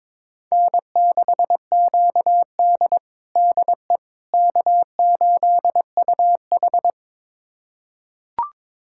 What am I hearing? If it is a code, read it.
Morse code, heard as a keyed tone, about 22 wpm, 700 Hz. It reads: N6QD DE K8UH
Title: USA Callsign DE Callsign - 4 Characters - 22wpm